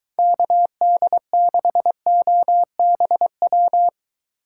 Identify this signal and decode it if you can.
Morse code, heard as a keyed tone, about 23 wpm, 700 Hz. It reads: KD6OBW